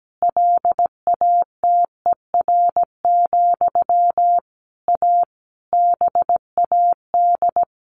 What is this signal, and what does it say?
Morse code, heard as a keyed tone, about 17 wpm, 700 Hz. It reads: LATER, A BAD
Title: A Sound Of Thunder 17wpm